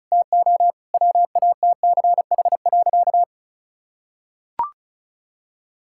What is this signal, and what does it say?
Morse code, heard as a keyed tone, about 34 wpm, 700 Hz. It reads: TO WATCH.
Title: Sentences from Top 1000 Words - Review Part 5 - 35wpm